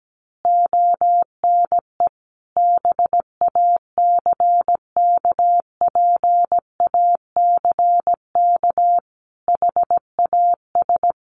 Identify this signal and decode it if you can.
Morse code, heard as a keyed tone, about 17 wpm, 700 Hz. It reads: ONE BACKPACK HAS
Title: Additional Easy Phrases 17wpm